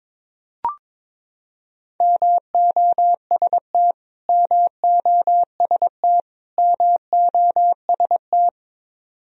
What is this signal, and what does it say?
Morse code, heard as a keyed tone, about 22 wpm, 700 Hz. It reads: MOST MOST MOST
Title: Top 100 Words Repeated 22wpm